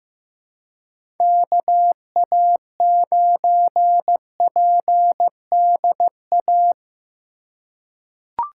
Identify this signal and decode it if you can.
Morse code, heard as a keyed tone, about 15 wpm, 700 Hz. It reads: KA9PDA
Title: USA Call Signs - 4x Character Spacing - 15wpm